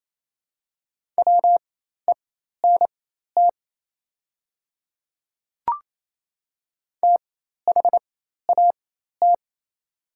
Morse code, heard as a keyed tone, about 28 wpm, spaced as for 13 wpm, 690 Hz. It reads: WENT THAT
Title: 4x Character Spacing - Top 200 Words - 28wpm